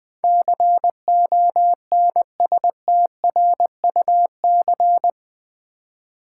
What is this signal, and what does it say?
Morse code, heard as a keyed tone, about 20 wpm, 700 Hz. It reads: CONSTRUC
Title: Build A Word - 4 Letters At A Time - 12-Letter Words - 20wpm